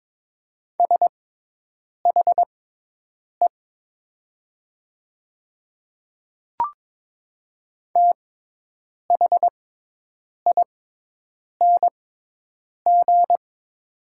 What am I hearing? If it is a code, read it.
Morse code, heard as a keyed tone, about 22 wpm, spaced as for 8 wpm, 700 Hz. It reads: SHE THING
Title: 6x Character Spacing - Top 200 Words - 22wpm